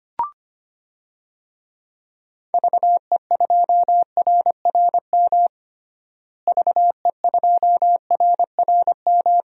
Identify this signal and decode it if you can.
Morse code, heard as a keyed tone, about 25 wpm, 700 Hz. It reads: VE2RRM VE2RRM
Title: DXCC-100 Call Signs - 6 Characters - Repeated 2x - 25wpm